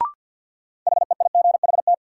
Morse code, heard as a keyed tone, about 50 wpm, 700 Hz. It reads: HEIGHT
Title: Sentences from Top 1600 Words - New Word Drill Alt - 50wpm